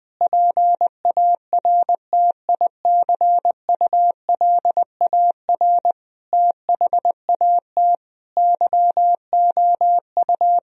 Morse code, heard as a keyed tone, about 20 wpm, 700 Hz. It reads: PARTICULAR THAT YOU
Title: Sentences from Top 1000 Words - Review Part 4 - 20wpm